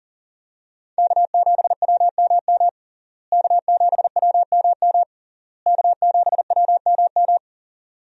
Morse code, heard as a keyed tone, about 37 wpm, 700 Hz. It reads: K7WMM K7WMM K7WMM
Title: USA Call Signs - 5 Characters - Repeated 3x - 40wpm